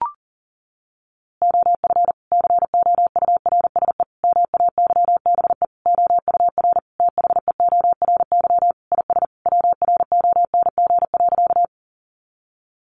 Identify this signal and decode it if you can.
Morse code, heard as a keyed tone, about 40 wpm, 700 Hz. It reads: OF COURSE MAYBE OUR THEORY IS WRONG.
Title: A Sound Of Thunder 40wpm